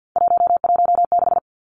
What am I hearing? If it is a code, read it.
Morse code, heard as a keyed tone, about 48 wpm, 700 Hz. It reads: 116
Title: Numbers - 3-Digit 50wpm